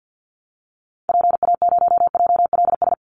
Morse code, heard as a keyed tone, about 49 wpm, 700 Hz. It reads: PA0JLS
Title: Call Signs Repeated 3x - 50wpm